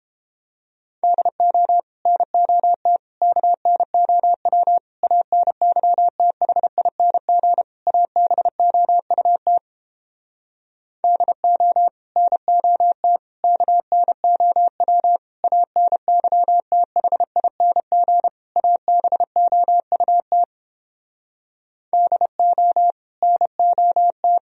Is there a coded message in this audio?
Yes: Morse code from 1.0 to 24.4 s, heard as a keyed tone, about 29 wpm, 700 Hz. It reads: DO NOT KNOW ANYTHING ABOUT DO NOT KNOW ANYTHING ABOUT DO NOT